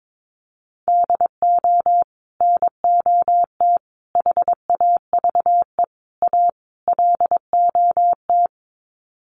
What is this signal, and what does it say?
Morse code, heard as a keyed tone, about 22 wpm, 700 Hz. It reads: DO NOT HAVE A LOT
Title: Sets of 5 Words Encore 22wpm